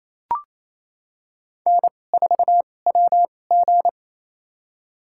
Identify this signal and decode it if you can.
Morse code, heard as a keyed tone, about 28 wpm, spaced as for 20 wpm, 700 Hz. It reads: N4WG
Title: USA Call Signs - 4 Characters - 2x Character Spacing - 28wpm